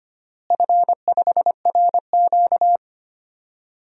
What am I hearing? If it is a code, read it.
Morse code, heard as a keyed tone, about 25 wpm, 700 Hz. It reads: F5RQ
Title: DXCC-100 Call Signs - 4 Characters - Repeated 3x - 25wpm